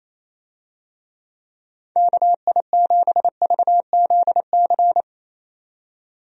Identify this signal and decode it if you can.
Morse code, heard as a keyed tone, about 27 wpm, 700 Hz. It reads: KI7VZC
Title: USA Call Signs - 6 Characters - 28wpm